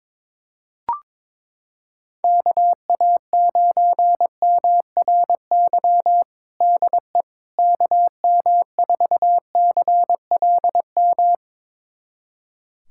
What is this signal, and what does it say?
Morse code, heard as a keyed tone, about 22 wpm, 700 Hz. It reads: KA9MRY DE KM4CLM
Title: USA Callsign DE Callsign - 6 Characters - 22wpm